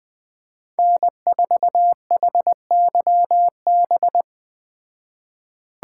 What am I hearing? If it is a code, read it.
Morse code, heard as a keyed tone, about 20 wpm, 700 Hz. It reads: N4HYB